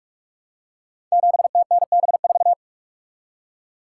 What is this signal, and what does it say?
Morse code, heard as a keyed tone, about 44 wpm, 700 Hz. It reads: 7TNB4
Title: Code Groups - 5 Letter:Numbers - 45wpm